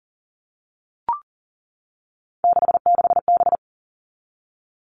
Morse code, heard as a keyed tone, about 40 wpm, 700 Hz. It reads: 66B
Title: All Cut Numbers - 3-Digit 40wpm